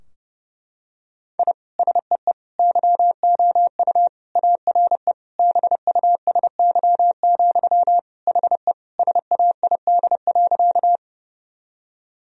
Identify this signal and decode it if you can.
Morse code, heard as a keyed tone, about 30 wpm, 700 Hz. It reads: I SEE YOU ARE BUSY, HE SAID.
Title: Sentences from Top 1000 Words - Review Part 3 - 30wpm